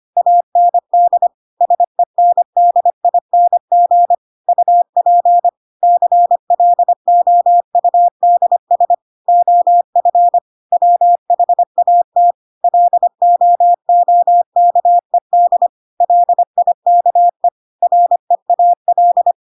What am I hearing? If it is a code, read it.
Morse code, heard as a keyed tone, about 25 wpm, 700 Hz. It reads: AND SENDING UP CLOUDS OF WHAT LOOKED LIKE REAL
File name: Pinocchio / Pinocchio ch3 25 25 ik1hge